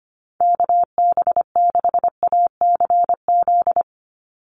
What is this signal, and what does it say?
Morse code, heard as a keyed tone, about 25 wpm, 700 Hz. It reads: KB6ACZ
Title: USA Call Signs 25wpm